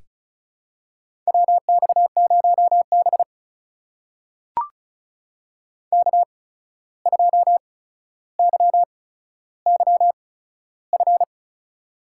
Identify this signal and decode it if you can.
Morse code, heard as a keyed tone, about 36 wpm, spaced as for 14 wpm, 700 Hz. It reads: WX0B K2YYF
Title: USA Call Signs - 8x Character Spacing - 35wpm